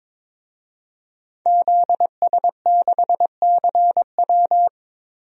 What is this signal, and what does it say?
Morse code, heard as a keyed tone, about 21 wpm, 700 Hz. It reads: ZS6CW
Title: DXCC-100 Call Signs - 5 Characters - Repeated 2x - 22wpm